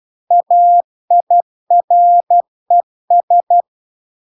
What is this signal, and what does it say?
Morse code, heard as a keyed tone, about 12 wpm, 700 Hz. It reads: AIRES